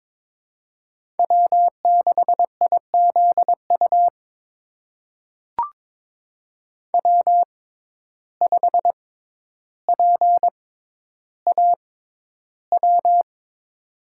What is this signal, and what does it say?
Morse code, heard as a keyed tone, about 22 wpm, spaced as for 11 wpm, 700 Hz. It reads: W6IZU W5PAW
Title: USA Call Signs - 5 Characters - 6x Character Spacing - 22wpm